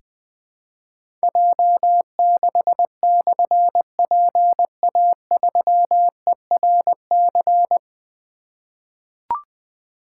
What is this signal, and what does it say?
Morse code, heard as a keyed tone, about 21 wpm, 700 Hz. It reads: J6/PA3ERC
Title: Difficult Call Signs 20wpm